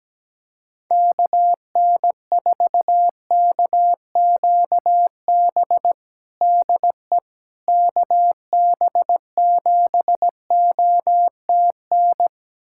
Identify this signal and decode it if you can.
Morse code, heard as a keyed tone, about 17 wpm, 700 Hz. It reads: KN4KQB DE KB7OTN
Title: USA Callsign DE Callsign 17wpm